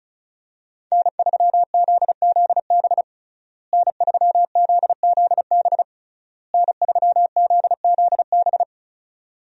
Morse code, heard as a keyed tone, about 36 wpm, 700 Hz. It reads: N3ZZB N3ZZB N3ZZB
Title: USA Call Signs - 5 Characters - Repeated 3x - 35wpm